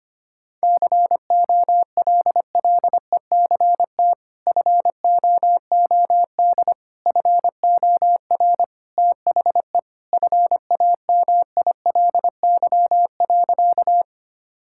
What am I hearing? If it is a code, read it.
Morse code, heard as a keyed tone, about 25 wpm, 700 Hz. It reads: COLLECT FOOD FOR THE FAMILY.
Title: Sentences from Top 1600 Words 25wpm